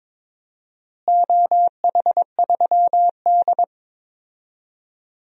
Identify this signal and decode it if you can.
Morse code, heard as a keyed tone, about 22 wpm, 700 Hz. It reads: OH3D
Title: DXCC-100 Call Signs - 4 Characters - Repeated 3x - 22wpm